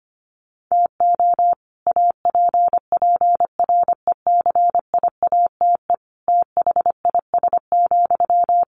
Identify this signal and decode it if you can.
Morse code, heard as a keyed tone, about 25 wpm, 700 Hz. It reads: TO APPRECIATE THIS,
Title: Sentences from Top 1300 Words 25wpm